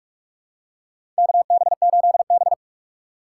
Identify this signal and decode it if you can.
Morse code, heard as a keyed tone, about 46 wpm, 700 Hz. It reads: KB8B
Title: USA Call Signs - 4x Character Spacing - 45wpm